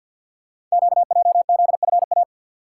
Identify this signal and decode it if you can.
Morse code, heard as a keyed tone, about 52 wpm, 700 Hz. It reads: QJZFA
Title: Code Groups - 5 Letters - 50wpm